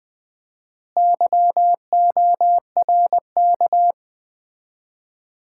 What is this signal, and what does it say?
Morse code, heard as a keyed tone, about 20 wpm, 700 Hz. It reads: YORK
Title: 4x Word-Spacing - Sets of 5 Words - 20wpm